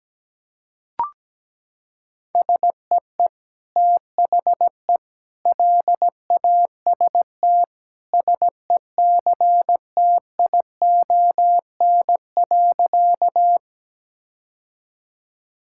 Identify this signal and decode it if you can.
Morse code, heard as a keyed tone, about 17 wpm, 700 Hz. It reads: SEE THE LAST SECTION.
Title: Sentences from Top 1000 Words - Review Part 5 - 17wpm